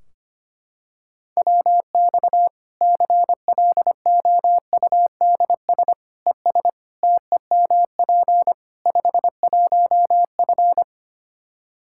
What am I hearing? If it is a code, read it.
Morse code, heard as a keyed tone, about 25 wpm, 700 Hz. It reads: WX CLOUDS ES TEMP 51F
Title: Practice QSO - Exchange 2 - Info - 25wpm